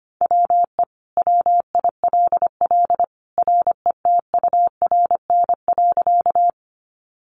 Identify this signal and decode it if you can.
Morse code, heard as a keyed tone, about 25 wpm, 700 Hz. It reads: WE WILL RETURN.
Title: Sentences from Top 800 Words 25wpm